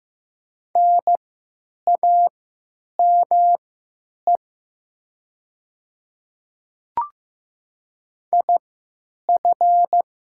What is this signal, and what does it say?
Morse code, heard as a keyed tone, about 15 wpm, spaced as for 9 wpm, 700 Hz. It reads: NAME IF
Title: 3x Character Spacing - Top 200 Words - 15wpm